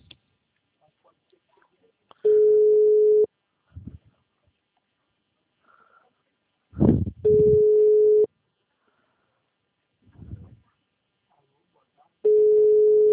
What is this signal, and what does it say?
Ringback tone (Europe, 425 Hz).
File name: out- -2016